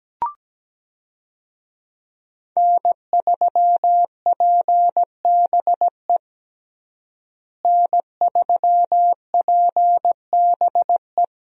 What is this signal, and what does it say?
Morse code, heard as a keyed tone, about 17 wpm, 700 Hz. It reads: N3PBE N3PBE
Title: USA Call Signs - 5 Characters - Repeated 3x - 17wpm